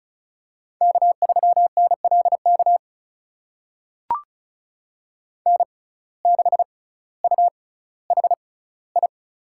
Morse code, heard as a keyed tone, about 36 wpm, spaced as for 11 wpm, 700 Hz. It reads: K3NL KN6UHI
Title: USA Call Signs - 6x Character Spacing - 35wpm